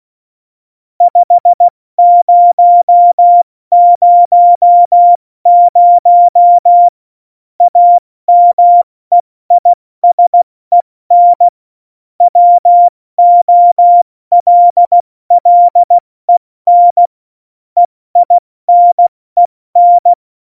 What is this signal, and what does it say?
Morse code, heard as a keyed tone, about 16 wpm, 700 Hz. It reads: 5000 AMEISEN WOLLEN EINEN
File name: Morsekurs / Witze / Witz 03.Juli 201, 70 F80